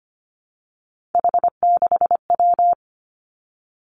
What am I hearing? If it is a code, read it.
Morse code, heard as a keyed tone, about 25 wpm, 700 Hz. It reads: H6W
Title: Three Challenging Letter-Number Combinations 25wpm